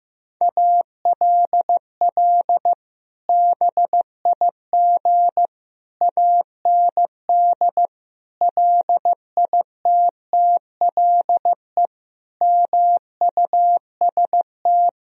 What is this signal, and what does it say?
Morse code, heard as a keyed tone, about 15 wpm, 700 Hz. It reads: ALL BIG AND LITTLE MUST
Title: Sentences from Top 1000 Words - Review Part 1 - 15wpm